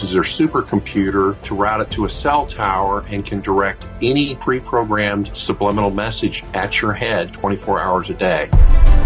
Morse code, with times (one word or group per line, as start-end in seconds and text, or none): none